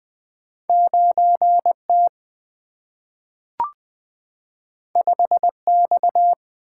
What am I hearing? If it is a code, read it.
Morse code, heard as a keyed tone, about 20 wpm, 700 Hz. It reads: 9 T 5 X
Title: Single-Letter Single-Number 20wpm